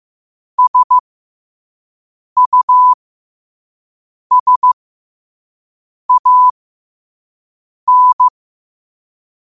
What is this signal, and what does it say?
Morse code, heard as a keyed tone, about 15 wpm, spaced as for 5 wpm, 1000 Hz. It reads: SUSAN
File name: qotdfiles / 5wpm